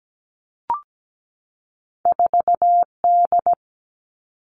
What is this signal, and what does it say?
Morse code, heard as a keyed tone, about 17 wpm, 700 Hz. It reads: 4D
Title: All Cut Numbers - 2-Digit 17wpm